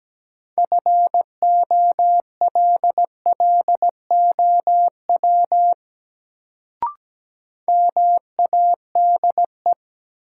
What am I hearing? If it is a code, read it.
Morse code, heard as a keyed tone, about 17 wpm, 700 Hz. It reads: FOLLOW MADE
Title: Sentences from Top 200 Words - New Word Drill - 17wpm